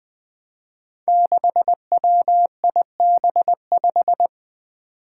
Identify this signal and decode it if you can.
Morse code, heard as a keyed tone, about 20 wpm, 700 Hz. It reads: 6WIB5
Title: Five Challenging Letter-Number Combinations 20wpm